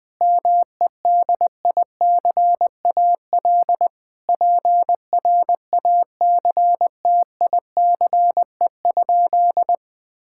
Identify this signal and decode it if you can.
Morse code, heard as a keyed tone, about 20 wpm, 700 Hz. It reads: MEDICAL PRACTICE?
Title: Sentences from Top 1000 Words - Review Part 5 - 20wpm